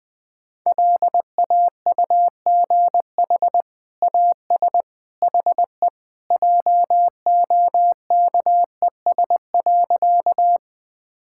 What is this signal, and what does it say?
Morse code, heard as a keyed tone, about 21 wpm, 700 Hz. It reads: LAUGH AS HE JOKES.
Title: Sentences with 2-Word Phrases - Spoken Beginning - 20wpm